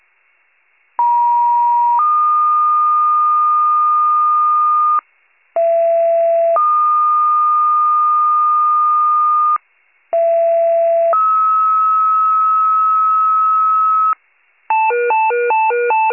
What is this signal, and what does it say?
Two-tone page, heard as a one-second tone, then a three-second tone, about 950 then 1220 Hz, about 670 then 1220 Hz, about 670 then 1290 Hz. It is heard from 1.0 to 14.1 s.